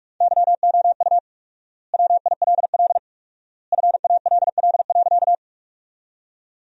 Morse code, heard as a keyed tone, about 47 wpm, 700 Hz. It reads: YOU WILL FALL.
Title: 4x Word-Spacing - Sentences from Top 500 Words - 45wpm